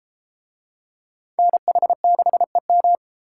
Morse code, heard as a keyed tone, about 32 wpm, 700 Hz. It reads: NH6EM